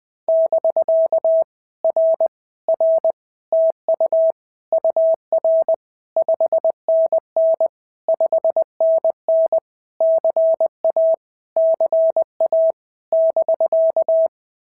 Morse code, heard as a keyed tone, about 20 wpm, 650 Hz. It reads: <BK> R R TU UR 5NN 5NN CA CA <BK>
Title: POTA QSO Practice - 20wpm